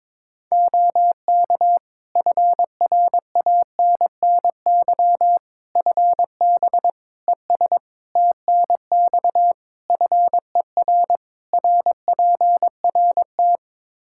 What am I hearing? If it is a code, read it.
Morse code, heard as a keyed tone, about 22 wpm, 700 Hz. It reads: OK FRANNY FB ES TNX FER RPRT
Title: Practice QSO - Exchange 2 - Info - 22wpm